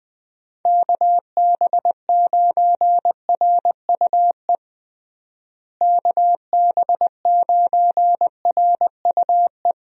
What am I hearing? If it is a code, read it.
Morse code, heard as a keyed tone, about 20 wpm, 700 Hz. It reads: KB9RUE KB9RUE
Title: USA Call Signs - 6 Characters - Repeated 3x - 20wpm